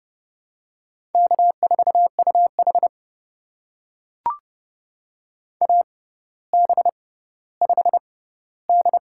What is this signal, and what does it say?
Morse code, heard as a keyed tone, about 30 wpm, spaced as for 15 wpm, 700 Hz. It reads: K4UH AB5D
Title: USA Call Signs - 4 Characters - 6x Character Spacing - 30wpm